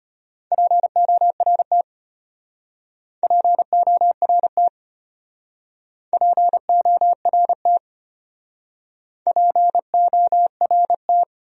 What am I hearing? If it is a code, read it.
Morse code, heard as a keyed tone, about 35 wpm, 700 Hz. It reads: PORT PORT PORT PORT